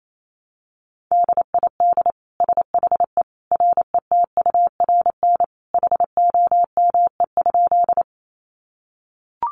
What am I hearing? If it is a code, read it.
Morse code, heard as a keyed tone, about 28 wpm, 700 Hz. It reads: DID SHE RETURN HOME?
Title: Sentences from Top 800 Words 28wpm